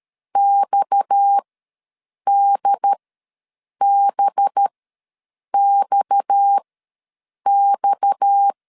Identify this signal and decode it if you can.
Morse code, heard as a keyed tone, about 13 wpm, spaced as for 7 wpm, 800 Hz. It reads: XDBXX
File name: LEZIONI / 6 PKX0 / 6 PKX ws 12wpm 3106